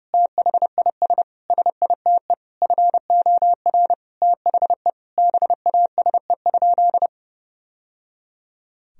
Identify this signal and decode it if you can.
Morse code, heard as a keyed tone, about 30 wpm, 700 Hz. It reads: THIS SITE FOR THE BASE?
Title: Sentences from Top 1000 Words - Review Part 1 - 30wpm